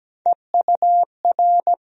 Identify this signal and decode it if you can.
Morse code, heard as a keyed tone, about 17 wpm, 700 Hz. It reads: EUR
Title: USA Call Sign Postfixes - 3 Letters 17wpm